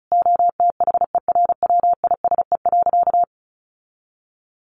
Morse code, heard as a keyed tone, about 34 wpm, 700 Hz. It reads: OTHERWISE.
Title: Sentences from Top 1200 Words 35wpm